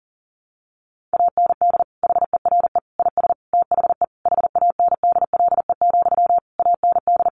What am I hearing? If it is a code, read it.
Morse code, heard as a keyed tone, about 37 wpm, 700 Hz. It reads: AND HERE IS THE HANDLE, AND
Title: Sentences from Top 1100 Words 40wpm